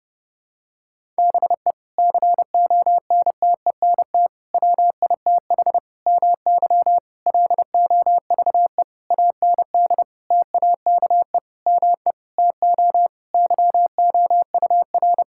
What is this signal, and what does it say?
Morse code, heard as a keyed tone, about 30 wpm, 700 Hz. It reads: BE CONTENT WITH MY LOVE AND TAKE ME TO YOUR
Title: Sentences from Top 1000 Words - Review Part 6 - 30wpm